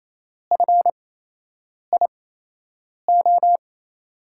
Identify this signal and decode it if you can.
Morse code, heard as a keyed tone, about 27 wpm, spaced as for 8 wpm, 700 Hz. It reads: FIO
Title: USA Call Signs - 6 Characters - 8x Character Spacing - 28wpm